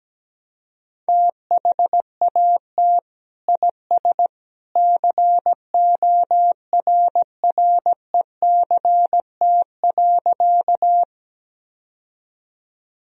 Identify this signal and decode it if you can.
Morse code, heard as a keyed tone, about 17 wpm, 700 Hz. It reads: THAT IS CORRECT.
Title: Sentences from Top 1000 Words - Review Part 2 - 17wpm